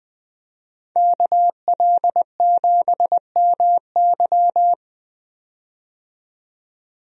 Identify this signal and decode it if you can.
Morse code, heard as a keyed tone, about 21 wpm, 700 Hz. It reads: KL7MY
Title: USA Call Signs - 5 Characters - 20wpm